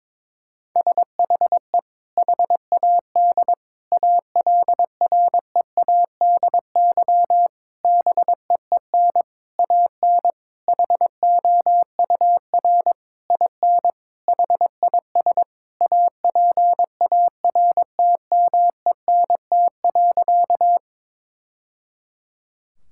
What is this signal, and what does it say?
Morse code, heard as a keyed tone, about 22 wpm, 700 Hz. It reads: SHE HAD ALREADY BEEN AN HOUR IN HIS APARTMENT.
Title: Sentences from Top 1000 Words - Review Part 6 - 22wpm